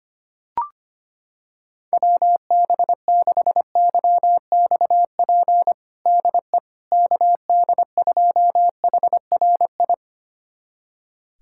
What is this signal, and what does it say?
Morse code, heard as a keyed tone, about 25 wpm, 700 Hz. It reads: WB6YXP DE KD2HRI
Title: USA Callsign DE Callsign 25wpm